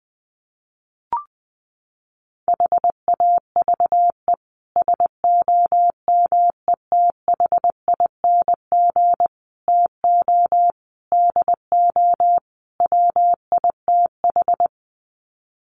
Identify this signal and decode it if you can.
Morse code, heard as a keyed tone, about 20 wpm, 700 Hz. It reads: HAVE SOMETHING TO DO WITH
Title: Sets of 5 Words Encore 20wpm